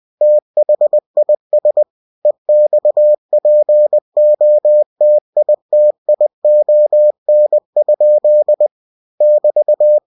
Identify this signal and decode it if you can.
Morse code, heard as a keyed tone, about 20 wpm, 600 Hz. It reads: THIS EXPOTITION? <BT>